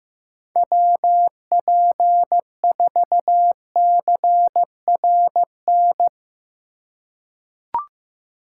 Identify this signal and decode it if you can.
Morse code, heard as a keyed tone, about 15 wpm, 700 Hz. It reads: WP4CRN
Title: USA Call Signs - 6 Characters - 3x Character Spacing - 15wpm